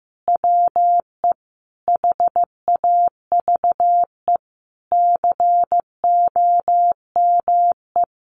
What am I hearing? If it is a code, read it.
Morse code, heard as a keyed tone, about 14 wpm, 700 Hz. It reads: WE HAVE COME
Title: Sentences from Top 1000 Words 15wpm